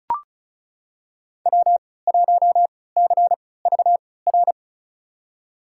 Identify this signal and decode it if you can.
Morse code, heard as a keyed tone, about 34 wpm, spaced as for 20 wpm, 700 Hz. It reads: W1CVR
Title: USA Call Signs - 5 Characters - 3x Character Spacing - 35wpm